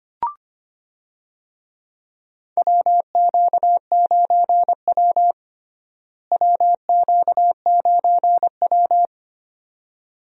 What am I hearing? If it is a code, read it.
Morse code, heard as a keyed tone, about 25 wpm, 700 Hz. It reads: WQ9W WQ9W
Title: USA Call Signs - 4 Characters - Repeated 2x - 25wpm